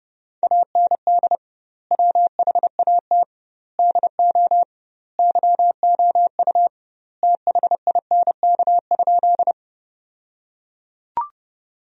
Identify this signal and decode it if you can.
Morse code, heard as a keyed tone, about 32 wpm, 700 Hz. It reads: AND WHAT DO YOU THINK?
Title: 2x Word-Spacing - Sentences from Top 200 Words - 30wpm